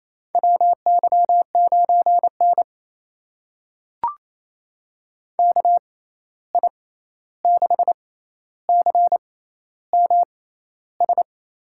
Morse code, heard as a keyed tone, about 28 wpm, spaced as for 13 wpm, 700 Hz. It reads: WY9N KI6CMS
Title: USA Call Signs - 6x Character Spacing - 28wpm